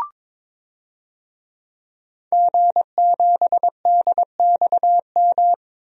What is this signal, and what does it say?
Morse code, heard as a keyed tone, about 22 wpm, 700 Hz. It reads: G7DXM
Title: DXCC-100 Call Signs - 5 Characters - Repeated 2x - 22wpm